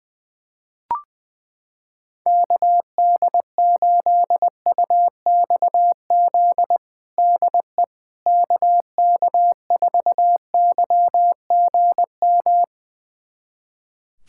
Morse code, heard as a keyed tone, about 21 wpm, 700 Hz. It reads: KD8UXZ DE KK4YGM
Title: USA Callsign DE Callsign - 6 Characters - 20wpm